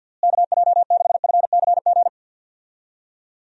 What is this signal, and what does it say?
Morse code, heard as a keyed tone, about 50 wpm, 700 Hz. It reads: KJ6FCG
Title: USA Call Signs - 6 Characters - Repeated 3x - 50wpm